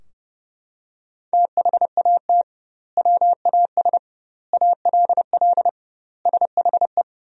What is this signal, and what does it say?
Morse code, heard as a keyed tone, about 30 wpm, 700 Hz. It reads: THAT WAS ALL SHE
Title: 2x Word-Spacing - Sentences from Top 100 Words - 30wpm